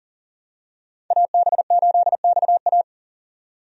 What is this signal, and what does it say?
Morse code, heard as a keyed tone, about 40 wpm, 700 Hz. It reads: AB8XA
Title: USA Call Signs - 5 Characters - 40wpm